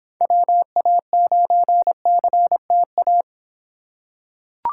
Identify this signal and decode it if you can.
Morse code, heard as a keyed tone, about 26 wpm, 700 Hz. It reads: WA9CTA